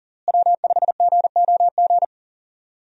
Morse code, heard as a keyed tone, about 38 wpm, 700 Hz. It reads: W5GOG
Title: USA Call Signs - 5 Characters - Repeated 2x - 40wpm